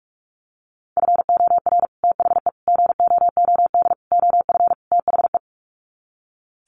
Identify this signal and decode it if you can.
Morse code, heard as a keyed tone, about 45 wpm, 700 Hz. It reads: FOR THE GOOD OF THE
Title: Sets of 5 Words Encore2 45wpm